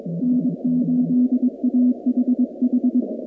Morse code, heard as a keyed tone, about 23 wpm, 250 Hz. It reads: N8AHH
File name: contests / 2019-cw-wpx-kt5j-k5pi- -2330 0-45